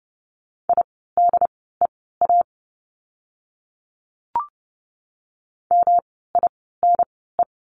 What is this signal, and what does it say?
Morse code, heard as a keyed tone, about 31 wpm, spaced as for 17 wpm, 700 Hz. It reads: IDEA MINE
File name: course / 30:10wpm / 108 Lesson 18f - Letter m - Words 30wpm